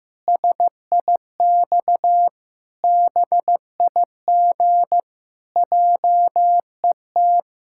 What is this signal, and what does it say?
Morse code, heard as a keyed tone, about 15 wpm, 700 Hz. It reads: SIX BIG JET
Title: Pangram Challenge v2 - 15wpm